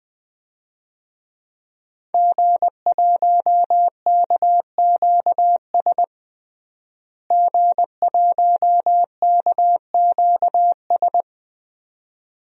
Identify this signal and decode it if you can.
Morse code, heard as a keyed tone, about 20 wpm, 700 Hz. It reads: G1KQS G1KQS
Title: DXCC-100 Call Signs - 5 Characters - Repeated 2x - 20wpm